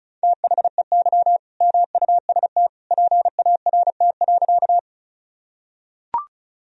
Morse code, heard as a keyed tone, about 36 wpm, 700 Hz. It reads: THEY MUST PART.
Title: Sentences from Top 1000 Words - Review Part 1 - 35wpm